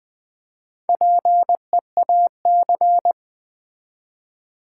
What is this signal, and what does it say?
Morse code, heard as a keyed tone, about 20 wpm, 700 Hz. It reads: PEAC
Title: Build A Word - 1 Letter At A Time - 3 to 5-Letter Words - 20wpm